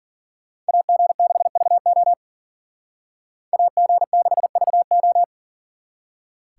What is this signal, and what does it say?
Morse code, heard as a keyed tone, about 52 wpm, 700 Hz. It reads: AG6VO AG6VO